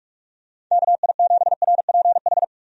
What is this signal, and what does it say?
Morse code, heard as a keyed tone, about 44 wpm, 700 Hz. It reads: KI7RPH
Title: USA Call Signs - 6 Characters - 45wpm